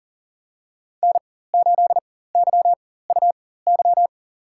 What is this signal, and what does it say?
Morse code, heard as a keyed tone, about 37 wpm, spaced as for 18 wpm, 700 Hz. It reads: N8YUY
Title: USA Call Signs - 5 Characters - 4x Character Spacing - 40wpm